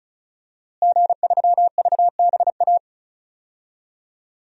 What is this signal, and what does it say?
Morse code, heard as a keyed tone, about 36 wpm, 700 Hz. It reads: G3VBA
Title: DXCC-100 Call Signs - 5 Characters - Repeated 3x - 35wpm